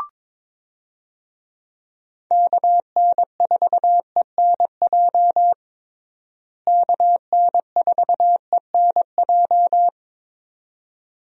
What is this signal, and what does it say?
Morse code, heard as a keyed tone, about 22 wpm, 700 Hz. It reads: KN4ENJ KN4ENJ
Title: USA Call Signs - 6 Characters - Repeated 2x - 22wpm